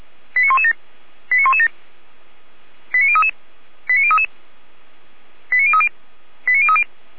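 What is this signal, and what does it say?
Five-tone selective call (ZVEI), heard as quick runs of five tones, tones of about 70 ms. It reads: 89117 89117 89033 89033 89030 89030
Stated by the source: ZVEI five-tone selective call 89503 received on SDR.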